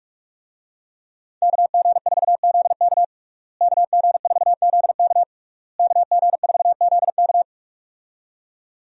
Morse code, heard as a keyed tone, about 46 wpm, 700 Hz. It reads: KG4ZK KG4ZK KG4ZK
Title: USA Call Signs - 5 Characters - Repeated 3x - 45wpm